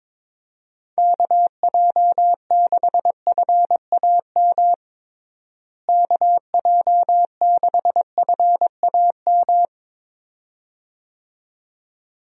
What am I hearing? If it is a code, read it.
Morse code, heard as a keyed tone, about 22 wpm, 700 Hz. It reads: KJ6FAM KJ6FAM
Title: USA Call Signs - 6 Characters - Repeated 2x - 22wpm